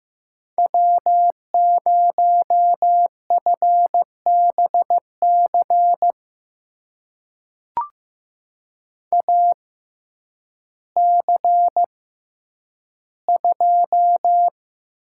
Morse code, heard as a keyed tone, about 15 wpm, spaced as for 9 wpm, 700 Hz. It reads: W0FBC AC2
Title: USA Call Signs - 5 Characters - 6x Character Spacing - 15wpm